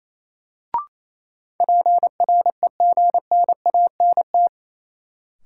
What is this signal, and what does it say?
Morse code, heard as a keyed tone, about 28 wpm, 700 Hz. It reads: PREGNANT
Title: Sentences from Top 1000 Words - Review Part 6 - New Word Drill - 28wpm